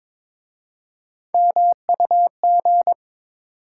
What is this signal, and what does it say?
Morse code, heard as a keyed tone, about 22 wpm, 700 Hz. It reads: MUG
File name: Difficult Words with Hints - 22wpm